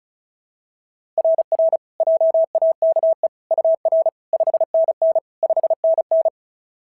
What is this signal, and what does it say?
Morse code, heard as a keyed tone, about 33 wpm, 660 Hz. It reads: RR JAKE UR 5NN 5NN
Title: SOTA QSO Practice 35wpm